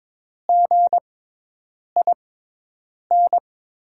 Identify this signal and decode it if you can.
Morse code, heard as a keyed tone, about 22 wpm, spaced as for 8 wpm, 700 Hz. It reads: GIN
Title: USA Call Signs - 5 Characters - 6x Character Spacing - 22wpm